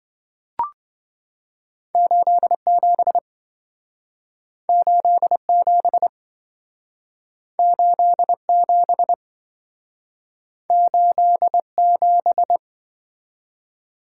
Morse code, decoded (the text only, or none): 87 87 87 87